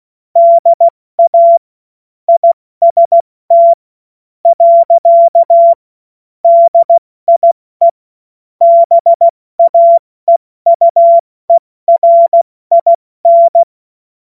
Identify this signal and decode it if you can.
Morse code, heard as a keyed tone, about 16 wpm, spaced as for 13 wpm, 680 Hz. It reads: DA IST. DIE BAEUERIN
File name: Morsekurs / Witze / Witz 27 Mai 2025 70 F80